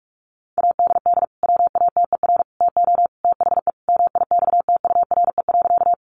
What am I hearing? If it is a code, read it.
Morse code, heard as a keyed tone, about 44 wpm, 700 Hz. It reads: ADD WATER TO THE MIXTURE.
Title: Sentences from Top 1500 Words 45wpm